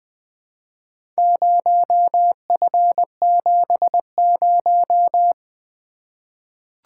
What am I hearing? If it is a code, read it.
Morse code, heard as a keyed tone, about 20 wpm, 700 Hz. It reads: 0F70
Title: Four Letter-Number Combinations - 20wpm